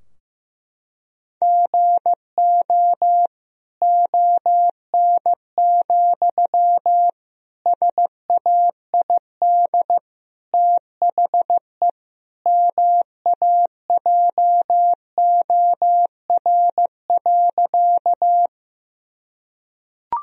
Morse code, heard as a keyed tone, about 16 wpm, 700 Hz. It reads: GO ON, SAID THE MAJOR.
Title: Sentences from Top 1000 Words - Review Part 4 - 15wpm